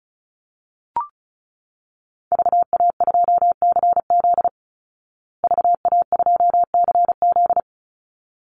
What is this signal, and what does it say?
Morse code, heard as a keyed tone, about 34 wpm, 700 Hz. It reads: VA2CZ VA2CZ
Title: Call Signs Repeated 3x - 35wpm